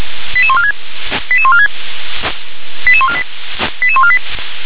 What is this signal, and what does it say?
Five-tone selective call (ZVEI), heard as quick runs of five tones, tones of about 70 ms. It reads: 88146 88146 88148 88148